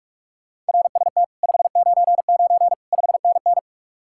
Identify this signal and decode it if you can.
Morse code, heard as a keyed tone, about 59 wpm, 700 Hz. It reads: RST 599 5NN